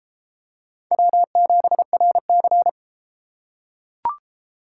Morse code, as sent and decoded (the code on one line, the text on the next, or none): .-- --... .-. -.-.
W7RC